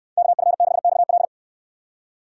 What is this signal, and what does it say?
Morse code, heard as a keyed tone, about 69 wpm, 700 Hz. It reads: KF6XL